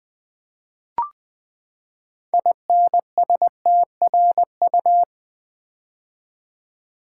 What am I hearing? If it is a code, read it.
Morse code, heard as a keyed tone, about 20 wpm, 700 Hz. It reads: INSTRU
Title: Build A Word - 6 Letters At A Time - 12-Letter Words - 20wpm